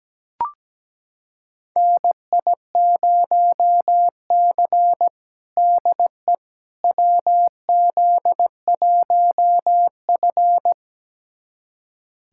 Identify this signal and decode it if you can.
Morse code, heard as a keyed tone, about 17 wpm, 700 Hz. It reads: NI0C DE WZ1F
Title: USA Callsign DE Callsign - 4 Characters - 17wpm